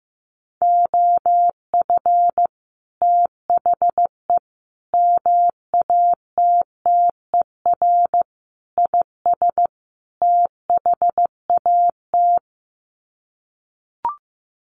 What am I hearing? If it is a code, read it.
Morse code, heard as a keyed tone, about 14 wpm, 700 Hz. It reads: OF THE MATTER IS THAT
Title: Sets of 5 Words Encore 15wpm